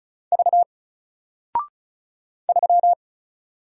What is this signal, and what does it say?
Morse code, heard as a keyed tone, about 35 wpm, 700 Hz. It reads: V 3
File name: Single-Letter-Number-35wpm